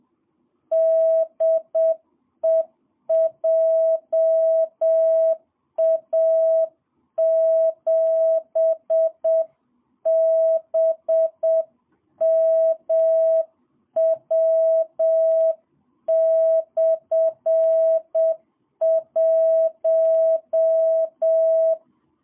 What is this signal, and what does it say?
Morse code, heard as a keyed tone, about 7 wpm, 660 Hz. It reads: DE JA7BMW/1
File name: jh8cbh / 2022cw